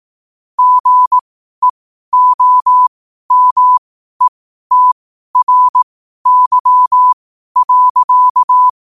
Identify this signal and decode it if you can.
Morse code, heard as a keyed tone, about 18 wpm, spaced as for 12 wpm, 1000 Hz. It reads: GEOMETRY.
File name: qotdfiles / 12wpm